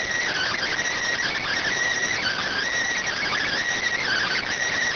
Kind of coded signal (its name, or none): Emergency Alert System (SAME) tones